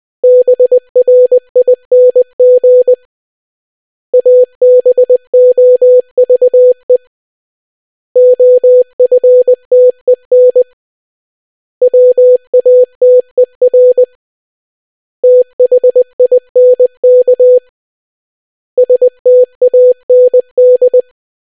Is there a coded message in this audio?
Yes: Morse code from 0.2 to 21.0 s, heard as a keyed tone, about 20 wpm, 500 Hz. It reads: BRING ABOVE OFTEN WATER THINK STAND